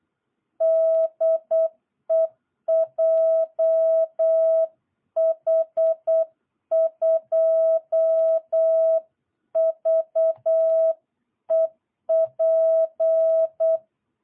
Morse code, heard as a keyed tone, about 8 wpm, 660 Hz. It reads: DE JH2VEP
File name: jh8cbh / 2022cw